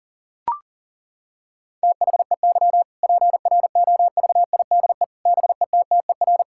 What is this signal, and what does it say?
Morse code, heard as a keyed tone, about 40 wpm, 700 Hz. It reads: THEY PROVIDE BETTER
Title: Sentences from Top 1000 Words - Review Part 5 - 40wpm